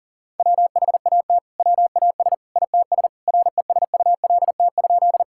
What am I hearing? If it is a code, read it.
Morse code, heard as a keyed tone, about 40 wpm, 700 Hz. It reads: WHAT WAS ITS RESULT?
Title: Sentences from Top 1000 Words - Review Part 5 - 40wpm